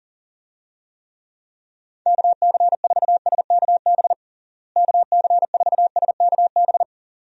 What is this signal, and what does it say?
Morse code, heard as a keyed tone, about 37 wpm, 700 Hz. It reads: KC4SKB KC4SKB
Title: USA Call Signs - 6 Characters - Repeated 2x - 40wpm